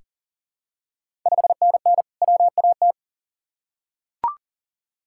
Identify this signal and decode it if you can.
Morse code, heard as a keyed tone, about 40 wpm, 690 Hz. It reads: 5NN WAT
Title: Ontario QSO Party Exchange 40wpm